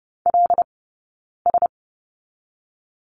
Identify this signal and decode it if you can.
Morse code, heard as a keyed tone, about 30 wpm, 700 Hz. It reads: L S
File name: course / 30:10wpm / 086 Lesson 15d - Number 5 - Two Characters 30wpm